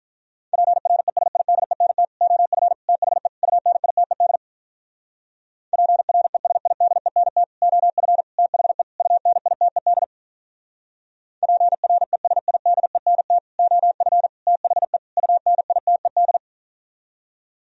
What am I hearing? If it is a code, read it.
Morse code, heard as a keyed tone, about 47 wpm, 700 Hz. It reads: PRESIDENT OF THE UNITED PRESIDENT OF THE UNITED PRESIDENT OF THE UNITED